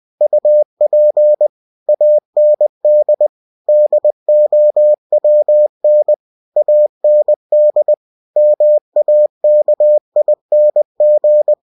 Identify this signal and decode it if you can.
Morse code, heard as a keyed tone, about 20 wpm, 600 Hz. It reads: UP AND DOWN AND MAKING
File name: winnie-20wpm